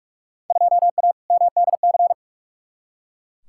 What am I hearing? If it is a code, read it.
Morse code, heard as a keyed tone, about 42 wpm, 700 Hz. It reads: 2A MDC
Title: Field Day Exchanges Encore - 45wpm